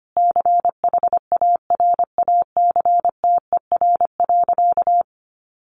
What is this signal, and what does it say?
Morse code, heard as a keyed tone, about 25 wpm, 700 Hz. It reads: CHARACTER.
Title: Sentences from Top 1200 Words 25wpm